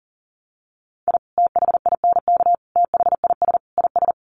Morse code, heard as a keyed tone, about 40 wpm, 700 Hz. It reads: I THINK THIS IS
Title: Sets of 4 Words 40wpm